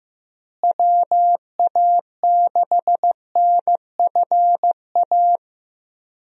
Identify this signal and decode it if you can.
Morse code, heard as a keyed tone, about 15 wpm, 700 Hz. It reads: WA6NFA
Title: USA Call Signs - 6 Characters - Repeated 2x - 15wpm